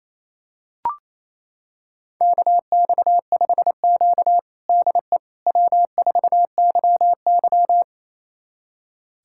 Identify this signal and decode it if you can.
Morse code, heard as a keyed tone, about 28 wpm, 700 Hz. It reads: KX5Q DE W4YY
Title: USA Callsign DE Callsign - 4 Characters - 28wpm